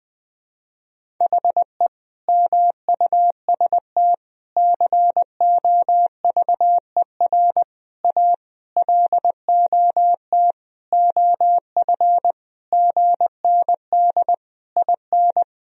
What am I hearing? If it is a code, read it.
Morse code, heard as a keyed tone, about 20 wpm, 700 Hz. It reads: HE MUST COVER A LOT OF GND IN
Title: Sentences from Top 500 Words and Abbreviations 20wpm